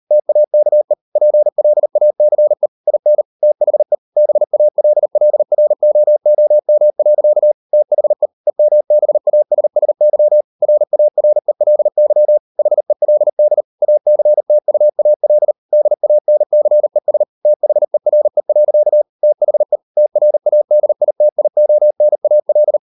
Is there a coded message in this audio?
Yes: Morse code from 0.1 to 22.8 s, heard as a keyed tone, about 38 wpm, 600 Hz. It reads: TAKE PLACE IN THE BALLROOM. THE EMBASSY RARELY HELD ACTUAL DANCES THERE. THE TRADITIONAL